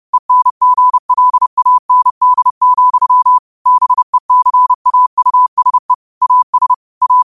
Morse code, heard as a keyed tone, about 30 wpm, 1000 Hz. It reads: ENGLAND, BECAUSE AS A
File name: qotdfiles / 30wpm